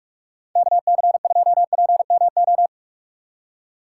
Morse code, heard as a keyed tone, about 49 wpm, 700 Hz. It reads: KC2PMO